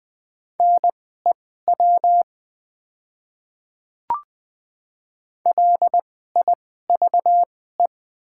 Morse code, heard as a keyed tone, about 20 wpm, spaced as for 14 wpm, 700 Hz. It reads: NEW LIVE
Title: 2x Character Spacing - Top 200 Words - 20wpm